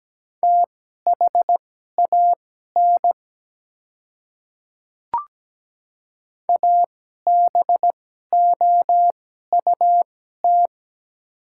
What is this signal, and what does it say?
Morse code, heard as a keyed tone, about 17 wpm, spaced as for 12 wpm, 700 Hz. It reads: THAN ABOUT
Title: 2x Character Spacing - Top 100 Words - 17wpm